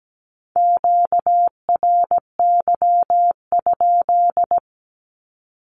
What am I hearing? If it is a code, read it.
Morse code, heard as a keyed tone, about 17 wpm, 700 Hz. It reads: QRY?
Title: Q-Codes Repeated 3x 17wpm